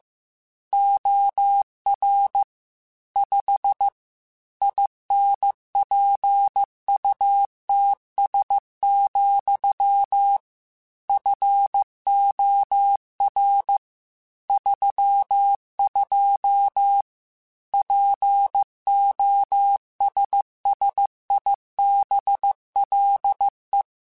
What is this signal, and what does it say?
Morse code, heard as a keyed tone, about 15 wpm, 800 Hz. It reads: OR 5 INPUTS, FOR 32 POSSIBLE
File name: Technical Subjects / Technical Subjects-mp3s / AND Gate @13